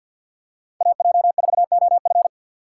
Morse code, heard as a keyed tone, about 52 wpm, 700 Hz. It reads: AJ4OF
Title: USA Call Signs - 5 Characters - 3x Character Spacing - 50wpm